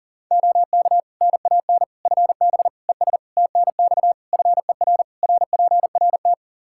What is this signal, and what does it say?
Morse code, heard as a keyed tone, about 37 wpm, 700 Hz. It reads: OK NAN FB ES TNX FER RPRT
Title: Practice QSO - Exchange 2 - Info - 40wpm